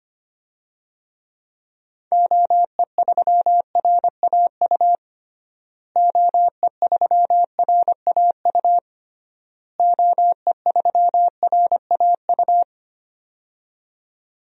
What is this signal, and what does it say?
Morse code, heard as a keyed tone, about 25 wpm, 700 Hz. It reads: OE3RAU OE3RAU OE3RAU
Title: DXCC-100 Call Signs - 6 Characters - Repeated 3x - 25wpm